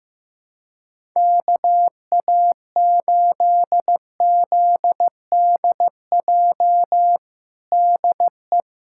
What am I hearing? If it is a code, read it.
Morse code, heard as a keyed tone, about 16 wpm, 700 Hz. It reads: KA8ZDJ DE
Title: USA Callsign DE Callsign 15wpm